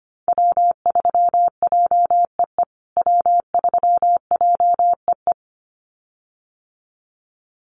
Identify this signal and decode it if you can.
Morse code, heard as a keyed tone, about 25 wpm, 700 Hz. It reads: W3JEE W3JEE
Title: USA Call Signs Repeated 2x 25wpm